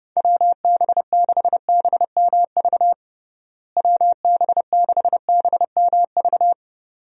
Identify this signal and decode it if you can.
Morse code, heard as a keyed tone, about 30 wpm, 700 Hz. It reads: WB6BMV WB6BMV
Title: USA Call Signs - 6 Characters - Repeated 2x - 30wpm